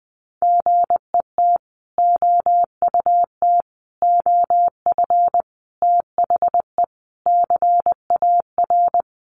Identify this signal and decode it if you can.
Morse code, heard as a keyed tone, about 19 wpm, 700 Hz. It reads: GET OUT OF THE CAR
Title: Sets of 5 Words Encore2 20wpm